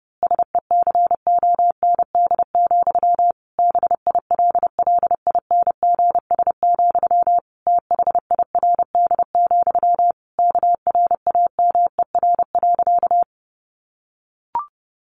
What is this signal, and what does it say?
Morse code, heard as a keyed tone, about 30 wpm, 700 Hz. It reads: SECOND, BILLINGS, THIRD, KRAMER.
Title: A Sound Of Thunder 30wpm